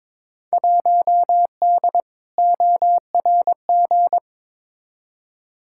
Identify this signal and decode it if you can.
Morse code, heard as a keyed tone, about 22 wpm, 700 Hz. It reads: 1D ORG
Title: Field Day Exchanges - 22wpm